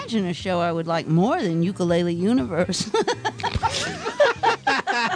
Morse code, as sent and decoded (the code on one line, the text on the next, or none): none